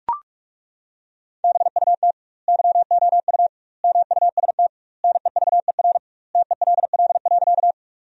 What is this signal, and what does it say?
Morse code, heard as a keyed tone, about 41 wpm, 700 Hz. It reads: BUT YOU MUST NEVER TELL.
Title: 2x Word-Spacing - Sentences from Top 300 Words - 45wpm